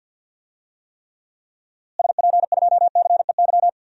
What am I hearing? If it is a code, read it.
Morse code, heard as a keyed tone, about 46 wpm, 700 Hz. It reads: SP2CEY
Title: DXCC-100 Call Signs - 6 Characters - Repeated 3x - 50wpm